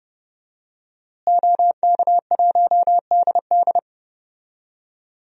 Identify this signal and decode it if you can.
Morse code, heard as a keyed tone, about 30 wpm, 700 Hz. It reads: OK1DD
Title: DXCC-100 Call Signs - 5 Characters - Repeated 2x - 30wpm